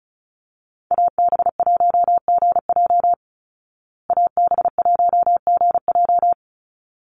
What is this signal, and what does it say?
Morse code, heard as a keyed tone, about 34 wpm, 700 Hz. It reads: AB1GJ AB1GJ
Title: Call Signs Repeated 3x - 35wpm